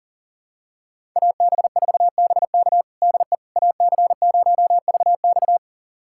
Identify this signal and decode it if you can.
Morse code, heard as a keyed tone, about 40 wpm, 700 Hz. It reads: AB4BK DE AC0VX
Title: USA Callsign DE Callsign - 5 Characters - 40wpm